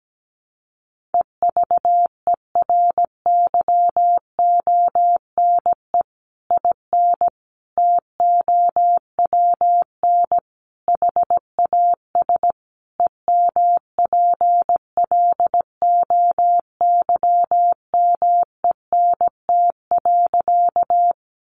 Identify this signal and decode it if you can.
Morse code, heard as a keyed tone, about 17 wpm, 700 Hz. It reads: EVERYONE IN TOWN HAS EMPLOYMENT.
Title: Sentences from Top 1500 Words 17wpm